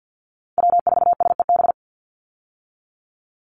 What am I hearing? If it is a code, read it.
Morse code, heard as a keyed tone, about 50 wpm, 700 Hz. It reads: W4SEB
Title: USA Call Signs 50wpm